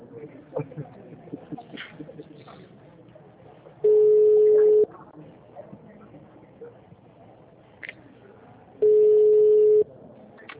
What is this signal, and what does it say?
Ringback tone (Europe, 425 Hz), with speech.